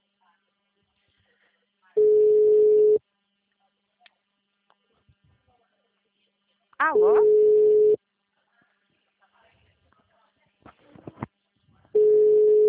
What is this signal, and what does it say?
Ringback tone (Europe, 425 Hz), with speech.